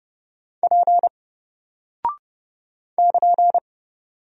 Morse code, heard as a keyed tone, about 30 wpm, 700 Hz. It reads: P <KN>
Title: Single Character (Letter:Number:Punctuation:Prosign) Alt 30wpm